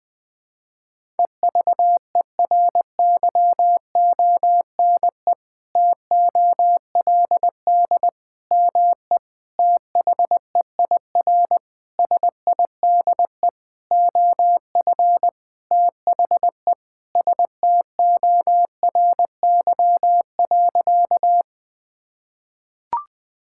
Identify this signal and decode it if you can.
Morse code, heard as a keyed tone, about 20 wpm, 700 Hz. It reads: EVERYONE TOLD ME THEIR SIDE OF THE STORY.
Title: Sentences from Top 1000 Words - Review Part 5 - 20wpm